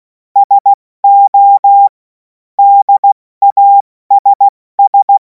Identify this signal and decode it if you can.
Morse code, heard as a keyed tone, about 16 wpm, spaced as for 14 wpm, 800 Hz. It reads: SO DASS
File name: Morsekurs / Witze / Witz 17. Februar 2026 70 F80